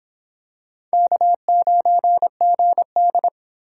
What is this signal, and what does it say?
Morse code, heard as a keyed tone, about 26 wpm, 700 Hz. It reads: K9GD